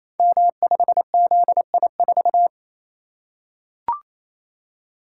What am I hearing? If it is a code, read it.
Morse code, heard as a keyed tone, about 29 wpm, 700 Hz. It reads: M5ZI4
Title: Code Groups - 5 Letter:Numbers - 28wpm